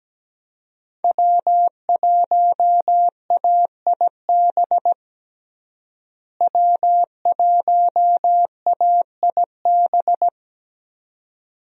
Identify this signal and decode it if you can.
Morse code, heard as a keyed tone, about 17 wpm, 700 Hz. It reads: W1AIB W1AIB
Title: USA Call Signs - 5 Characters - Repeated 2x - 17wpm